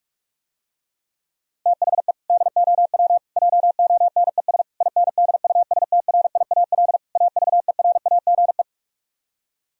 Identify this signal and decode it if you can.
Morse code, heard as a keyed tone, about 49 wpm, 700 Hz. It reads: THE DOW JONES INDUSTRIAL AVERAGE